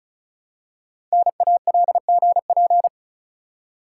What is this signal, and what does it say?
Morse code, heard as a keyed tone, about 36 wpm, 690 Hz. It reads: NALGP
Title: Code Groups - 5 Letters - 35wpm